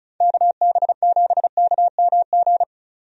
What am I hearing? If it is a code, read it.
Morse code, heard as a keyed tone, about 34 wpm, 700 Hz. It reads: KB7KMG
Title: USA Call Signs - 6 Characters - 35wpm